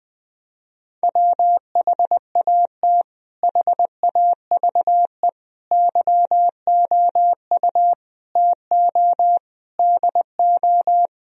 Morse code, heard as a keyed tone, about 20 wpm, 700 Hz. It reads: WHAT HAVE YOU TO DO
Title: Sentences from Top 1000 Words - Review Part 2 - 20wpm